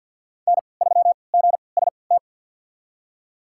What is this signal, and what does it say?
Morse code, heard as a keyed tone, about 51 wpm, spaced as for 28 wpm, 700 Hz. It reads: N3GST
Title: USA Call Signs - 5 Characters - 3x Character Spacing - 50wpm